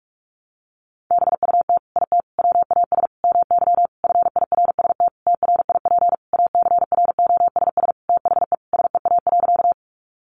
Morse code, heard as a keyed tone, about 44 wpm, 700 Hz. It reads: BUT IT WAS MY FIRST TRIP ACROSS THE SEA.
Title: Sentences from Top 1300 Words 45wpm